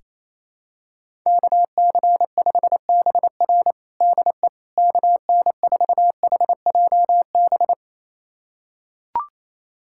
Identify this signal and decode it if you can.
Morse code, heard as a keyed tone, about 28 wpm, 700 Hz. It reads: KC5BR DE KN4HJB
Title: USA Callsign DE Callsign 28wpm